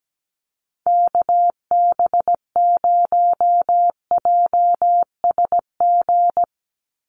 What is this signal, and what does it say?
Morse code, heard as a keyed tone, about 17 wpm, 700 Hz. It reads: KB0JSG
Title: USA Call Signs 17wpm